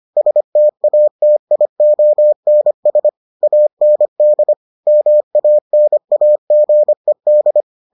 Morse code, heard as a keyed tone, about 25 wpm, 600 Hz. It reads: STATIONS AND MANAGED